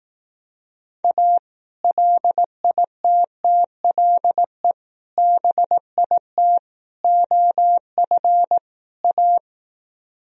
Morse code, heard as a keyed tone, about 18 wpm, 700 Hz. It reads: A LITTLE BIT OF A
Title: Speed-Racing - Sets of 5 Words 15wpm